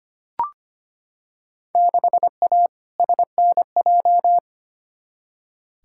Morse code, heard as a keyed tone, about 25 wpm, 700 Hz. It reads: 6A SNJ
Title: Field Day Exchanges Encore - 25wpm